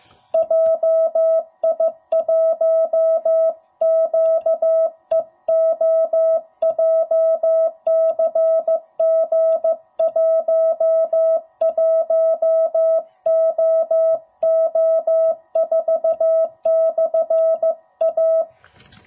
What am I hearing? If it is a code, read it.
Morse code, heard as a keyed tone, about 16 wpm, 650 Hz. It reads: JI1QEO JCG11OO4/A
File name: jh8cbh / 2021cw / cw05b